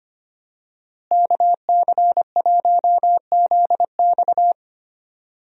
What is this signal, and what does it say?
Morse code, heard as a keyed tone, about 25 wpm, 690 Hz. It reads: KC1ZX
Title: USA Call Signs - 5 Characters - Repeated 2x - 25wpm